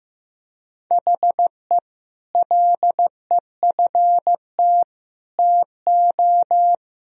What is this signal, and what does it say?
Morse code, heard as a keyed tone, about 15 wpm, 700 Hz. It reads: HE LEFT TO
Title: Sentences from Top 1000 Words - Review Part 2 - 15wpm